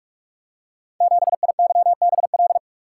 Morse code, heard as a keyed tone, about 44 wpm, 700 Hz. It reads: 7IYBL
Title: Code Groups - 5 Letter:Numbers - 45wpm